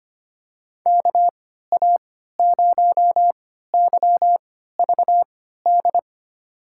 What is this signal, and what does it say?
Morse code, heard as a keyed tone, about 25 wpm, spaced as for 14 wpm, 700 Hz. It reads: KA0YVD
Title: USA Call Signs - 3x Character Spacing - 25wpm